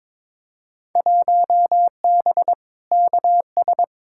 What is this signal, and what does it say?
Morse code, heard as a keyed tone, about 22 wpm, 700 Hz. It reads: 1B KS
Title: Field Day Exchanges Encore - 22wpm